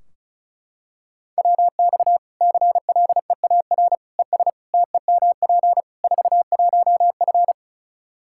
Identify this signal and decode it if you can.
Morse code, heard as a keyed tone, about 36 wpm, 700 Hz. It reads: WX CLEAR ES TEMP 41F
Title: K1USN SST QSO Practice 35wpm